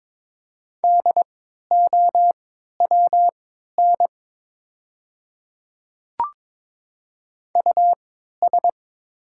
Morse code, heard as a keyed tone, about 22 wpm, spaced as for 12 wpm, 700 Hz. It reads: DOWN US
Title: 3x Character Spacing - Top 300 Words - 22wpm